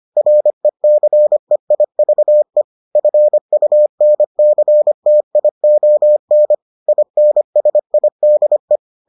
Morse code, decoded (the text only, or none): RECEIVE FUNCTION INSIDE